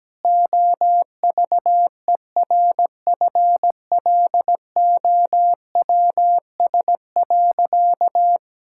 Morse code, heard as a keyed tone, about 17 wpm, 700 Hz. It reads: OVERFLOWS.
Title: Pangram Challenge v2 - 17wpm